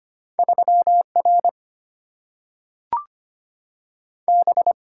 Morse code, heard as a keyed tone, about 25 wpm, 690 Hz. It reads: 3 R B
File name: Two Letter-Number Combinations - 25wpm